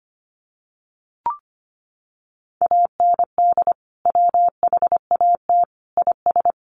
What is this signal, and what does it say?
Morse code, heard as a keyed tone, about 25 wpm, 700 Hz. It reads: AND WHAT IS
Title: Sentences from Top 100 Words 25wpm